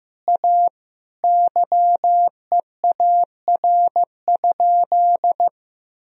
Morse code, heard as a keyed tone, about 15 wpm, 700 Hz. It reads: A YEAR?
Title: Sentences from Top 1000 Words - Review Part 2 - 15wpm